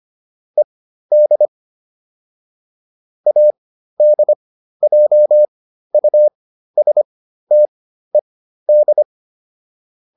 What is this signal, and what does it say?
Morse code, heard as a keyed tone, about 25 wpm, spaced as for 13 wpm, 600 Hz. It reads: ED ADJUSTED